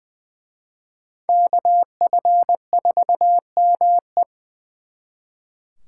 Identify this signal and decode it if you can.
Morse code, heard as a keyed tone, about 24 wpm, 700 Hz. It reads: KF4ME
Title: Speed Racing - USA 2x2 Call Signs 20wpm